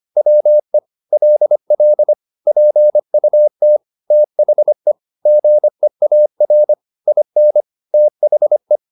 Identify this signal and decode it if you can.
Morse code, heard as a keyed tone, about 25 wpm, 600 Hz. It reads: WE LL PUT THE GEAR IN THE
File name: the old man and the sea part 1 25wpm